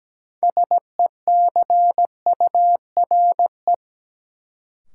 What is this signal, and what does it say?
Morse code, heard as a keyed tone, about 17 wpm, 700 Hz. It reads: SECURE
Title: Sentences from Top 1200 Words - New Word Drill Alt - 17wpm